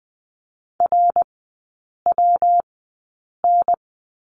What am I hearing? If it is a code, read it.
Morse code, heard as a keyed tone, about 20 wpm, spaced as for 8 wpm, 700 Hz. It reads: RWN
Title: 114 Lesson 19e - Letter w - Three Characters 20wpm